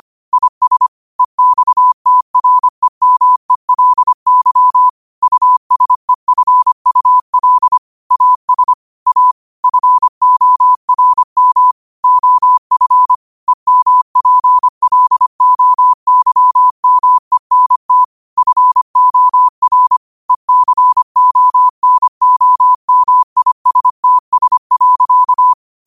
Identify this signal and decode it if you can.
Morse code, heard as a keyed tone, about 25 wpm, 1000 Hz. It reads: IS EXTREMELY USEFUL AS A FORM OF EMPLOYMENT FOR ECONOMISTS.